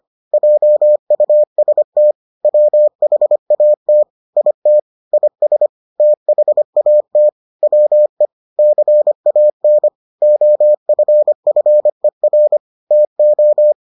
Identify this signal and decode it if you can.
Morse code, heard as a keyed tone, about 25 wpm, 600 Hz. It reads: JUST WHAT IT IS THAT WE CAN OFFER TO